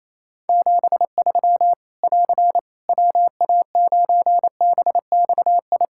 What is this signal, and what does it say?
Morse code, heard as a keyed tone, about 28 wpm, 700 Hz. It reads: 73 <AR> WA9BXI
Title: Practice QSO - Exchange 4 - Ending - 28wpm